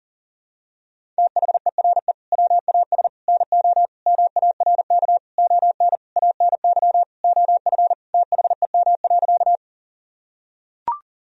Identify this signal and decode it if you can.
Morse code, heard as a keyed tone, about 40 wpm, 700 Hz. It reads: THERE WAS NO MARK ON ANY OF THEM.
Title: Sentences from Top 1000 Words - Review Part 1 - 40wpm